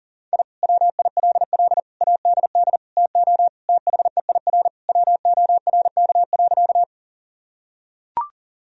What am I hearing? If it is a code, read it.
Morse code, heard as a keyed tone, about 40 wpm, 700 Hz. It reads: I WILL ADD TO THEIR WORK.
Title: Sentences from Top 1000 Words - Review Part 1 - 40wpm